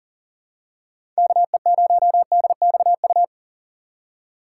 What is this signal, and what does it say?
Morse code, heard as a keyed tone, about 40 wpm, 700 Hz. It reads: KE0DXU
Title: USA Call Signs - 6 Characters - 3x Character Spacing - 40wpm